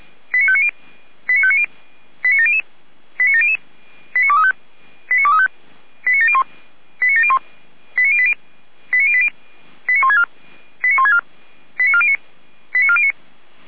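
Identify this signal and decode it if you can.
Five-tone selective call (ZVEI), heard as quick runs of five tones, tones of about 70 ms. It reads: 89590 89590 89700 89700 89235 89235 89701 89701 89080 89080 89164 89164 89409 89409
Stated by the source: Five-tone ZVEI selective call received on an SDR, channel or frequency 455.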